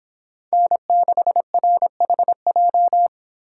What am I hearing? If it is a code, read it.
Morse code, heard as a keyed tone, about 26 wpm, 700 Hz. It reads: N6RHJ